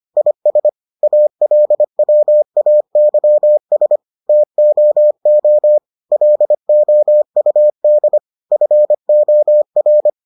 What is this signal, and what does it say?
Morse code, heard as a keyed tone, about 25 wpm, 600 Hz. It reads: IS ALWAYS TOO LOUD FOR